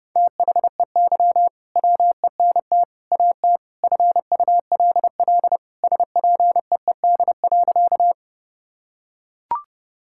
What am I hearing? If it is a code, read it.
Morse code, heard as a keyed tone, about 30 wpm, 700 Hz. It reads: THEY WENT AT FULL SPEED.
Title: Sentences from Top 1000 Words - Review Part 5 - 30wpm